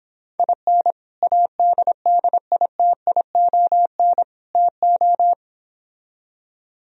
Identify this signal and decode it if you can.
Morse code, heard as a keyed tone, about 26 wpm, 700 Hz. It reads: IN ADDITION TO